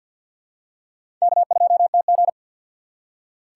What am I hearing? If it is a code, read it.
Morse code, heard as a keyed tone, about 48 wpm, 700 Hz. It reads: K2TG
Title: USA Call Signs - 4x Character Spacing - 50wpm